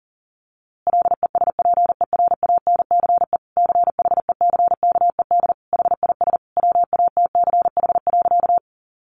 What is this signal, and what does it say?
Morse code, heard as a keyed tone, about 40 wpm, 700 Hz. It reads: LESPERANCE CHECKED HIS WATCH.
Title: A Sound Of Thunder 40wpm